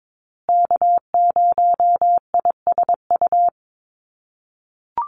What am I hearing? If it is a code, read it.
Morse code, heard as a keyed tone, about 22 wpm, 700 Hz. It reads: K0ISU
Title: USA Call Signs Repeated 2x 22wpm